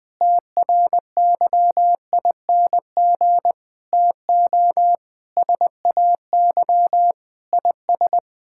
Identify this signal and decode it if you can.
Morse code, heard as a keyed tone, about 20 wpm, 700 Hz. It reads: TRYING TO SAY IS
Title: Sets of 8 Words - 20wpm